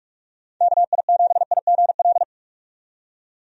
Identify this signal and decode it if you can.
Morse code, heard as a keyed tone, about 44 wpm, 700 Hz. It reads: KI7IGL
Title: USA Call Signs - 6 Characters - 45wpm